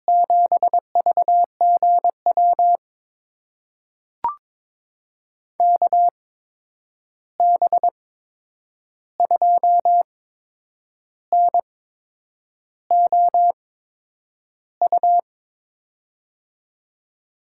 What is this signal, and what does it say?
Morse code, heard as a keyed tone, about 21 wpm, spaced as for 8 wpm, 700 Hz. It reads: 7VGW KB2NOU
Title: USA Call Signs - 6 Characters - 8x Character Spacing - 22wpm